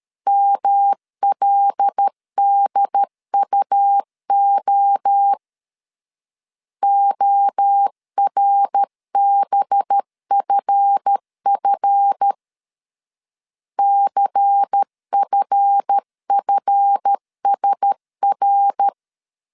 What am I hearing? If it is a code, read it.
Morse code, heard as a keyed tone, about 13 wpm, 800 Hz. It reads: MLDUO ORBFF CFFSR
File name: LEZIONI / 7 GF9Z / 7 GF 12wpm 2122